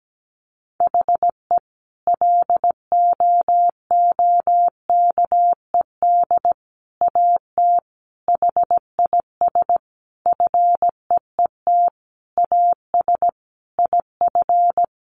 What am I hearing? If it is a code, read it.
Morse code, heard as a keyed tone, about 17 wpm, 700 Hz. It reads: HE LOOKED AT HIS FEET AS IF
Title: A Sound Of Thunder 17wpm